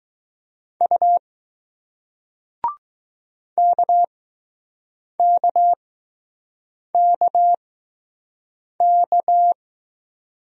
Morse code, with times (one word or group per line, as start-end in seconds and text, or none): 0.8-1.2 U
3.6-9.5 KKKK